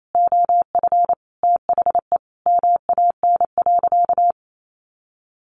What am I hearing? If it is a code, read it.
Morse code, heard as a keyed tone, about 28 wpm, 700 Hz. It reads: OF THE MAN.
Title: Sentences from Top 1500 Words 28wpm